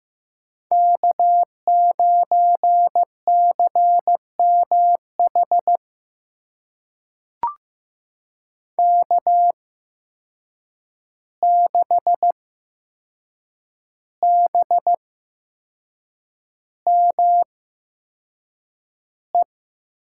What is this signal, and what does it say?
Morse code, heard as a keyed tone, about 15 wpm, spaced as for 6 wpm, 700 Hz. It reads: K9CMH K6BME
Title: USA Call Signs - 8x Character Spacing - 15wpm